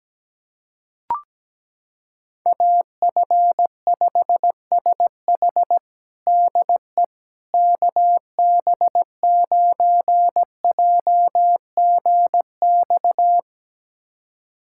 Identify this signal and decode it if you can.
Morse code, heard as a keyed tone, about 17 wpm, 700 Hz. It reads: AF5SH DE KB9JGX
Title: USA Callsign DE Callsign 17wpm